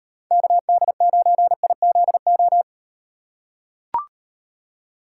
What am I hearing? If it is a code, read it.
Morse code, heard as a keyed tone, about 38 wpm, 700 Hz. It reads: KD9IZO